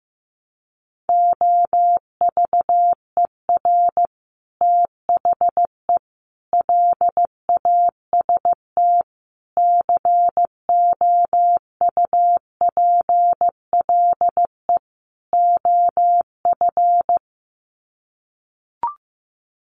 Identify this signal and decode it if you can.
Morse code, heard as a keyed tone, about 15 wpm, 700 Hz. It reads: OVER THE LAST COUPLE OF
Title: Sets of 5 Words Encore 15wpm